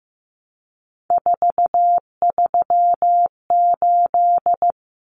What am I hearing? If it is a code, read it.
Morse code, heard as a keyed tone, about 15 wpm, 700 Hz. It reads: 438
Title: Common Cut Numbers - 3-Digit 15wpm